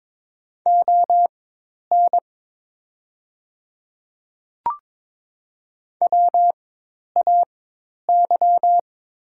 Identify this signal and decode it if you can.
Morse code, heard as a keyed tone, about 22 wpm, spaced as for 10 wpm, 700 Hz. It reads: ON WAY
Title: 4x Character Spacing - Top 200 Words - 22wpm